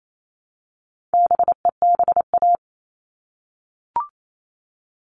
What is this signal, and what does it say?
Morse code, heard as a keyed tone, about 28 wpm, 700 Hz. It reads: BEBA
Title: All Cut Numbers - 4-Digit 28wpm